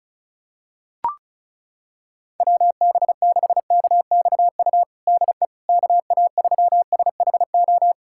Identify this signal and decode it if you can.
Morse code, heard as a keyed tone, about 36 wpm, 700 Hz. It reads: WB6KXU DE KA3SHO
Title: USA Callsign DE Callsign 35wpm